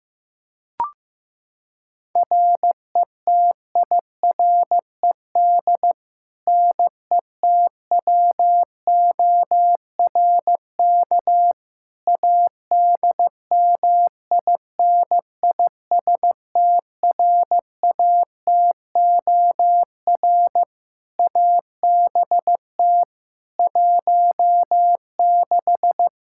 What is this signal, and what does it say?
Morse code, heard as a keyed tone, about 15 wpm, 700 Hz. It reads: RETIRED NETWORK ADMINISTRATOR ABT 16
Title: Practice QSO - Exchange 3 - More Info - 15wpm